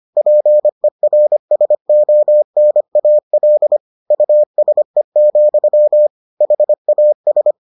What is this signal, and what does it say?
Morse code, heard as a keyed tone, about 25 wpm, 600 Hz. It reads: PERSONAL USE, HAS